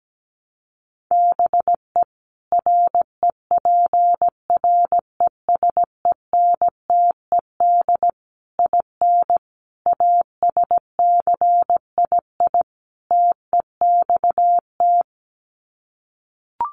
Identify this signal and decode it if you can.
Morse code, heard as a keyed tone, about 17 wpm, 700 Hz. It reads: BE REPRESENTED IN ASCII TEXT
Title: Sets of 5 Words 17wpm